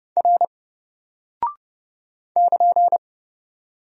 Morse code, heard as a keyed tone, about 30 wpm, 700 Hz. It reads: R <KN>
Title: Single Character (Letter:Number:Punctuation:Prosign) 30wpm